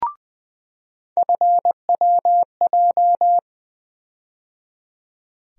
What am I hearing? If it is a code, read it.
Morse code, heard as a keyed tone, about 21 wpm, 700 Hz. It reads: FWJ
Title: USA Call Sign Postfixes - 3 Letters 20wpm